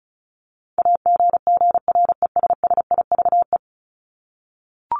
Morse code, heard as a keyed tone, about 34 wpm, 700 Hz. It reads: AGGRESSIVE
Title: Top 1501-1750 Words 35wpm